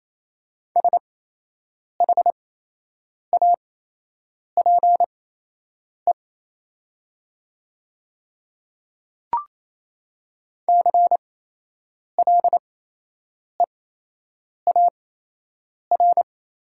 Morse code, heard as a keyed tone, about 28 wpm, spaced as for 8 wpm, 700 Hz. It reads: SHAPE CLEAR
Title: 8x Character Spacing - Top 500 Words - 28wpm